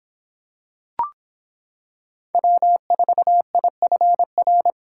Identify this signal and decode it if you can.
Morse code, heard as a keyed tone, about 26 wpm, 700 Hz. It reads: W4IFR